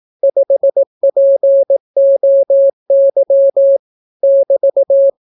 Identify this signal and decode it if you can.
Morse code, heard as a keyed tone, about 18 wpm, 550 Hz. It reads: 5POY <BT>